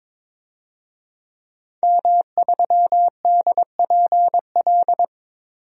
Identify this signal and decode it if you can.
Morse code, heard as a keyed tone, about 22 wpm, 700 Hz. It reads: M3DPL
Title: DXCC-100 Call Signs - 5 Characters - Repeated 3x - 22wpm